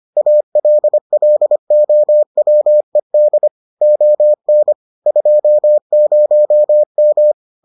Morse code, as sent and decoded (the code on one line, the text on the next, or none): .- .-.. .-.. --- .-- . -.. / --- -. / ..--- ----- --
ALLOWED ON 20M